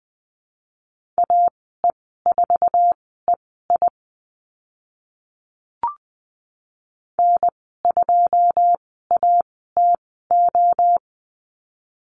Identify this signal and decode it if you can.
Morse code, heard as a keyed tone, about 19 wpm, spaced as for 14 wpm, 700 Hz. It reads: AE4EI N2ATO
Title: 082 Lesson 14g - Number 2 - Call Signs 20wpm